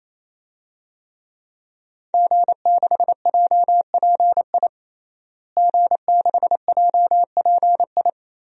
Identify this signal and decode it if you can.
Morse code, heard as a keyed tone, about 28 wpm, 700 Hz. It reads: G6JPI G6JPI
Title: DXCC-100 Call Signs - 5 Characters - Repeated 3x - 28wpm